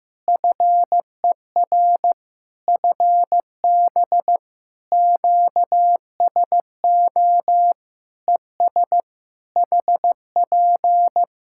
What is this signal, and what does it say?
Morse code, heard as a keyed tone, about 15 wpm, 700 Hz. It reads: FER FB QSO ES HP
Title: Practice QSO - Exchange 4 - Ending - 15wpm